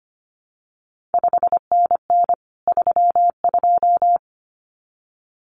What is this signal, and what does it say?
Morse code, heard as a keyed tone, about 25 wpm, 700 Hz. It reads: 5NN 32
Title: CQ WW DX Contest Exchange - 25wpm